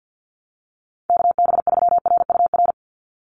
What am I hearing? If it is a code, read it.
Morse code, heard as a keyed tone, about 50 wpm, 700 Hz. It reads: KB3RUR
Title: USA Call Signs 50wpm